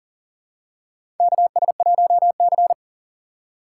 Morse code, heard as a keyed tone, about 38 wpm, 700 Hz. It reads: KS1C
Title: USA Call Signs - 4 Characters - 6x Character Spacing - 40wpm